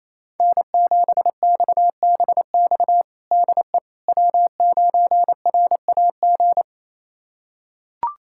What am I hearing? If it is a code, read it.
Morse code, heard as a keyed tone, about 27 wpm, 700 Hz. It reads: N7XBX DE W9RAG
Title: USA Callsign DE Callsign - 5 Characters - 28wpm